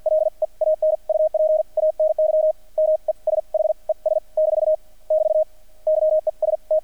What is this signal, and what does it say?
Morse code, heard as a keyed tone, about 50 wpm, spaced as for 40 wpm, 620 Hz. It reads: LEAN UP ANY MESHES - <BT> YES T
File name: archived / CRSnet